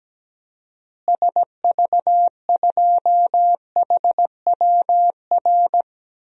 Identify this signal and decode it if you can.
Morse code, heard as a keyed tone, about 17 wpm, 700 Hz. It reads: SV2HWR
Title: DXCC-100 Call Signs - 6 Characters - Repeated 2x - 17wpm